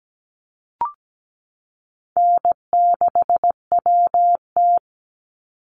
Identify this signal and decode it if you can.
Morse code, heard as a keyed tone, about 17 wpm, 700 Hz. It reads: N6WT
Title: NA SOTA Chaser Callsigns 17wpm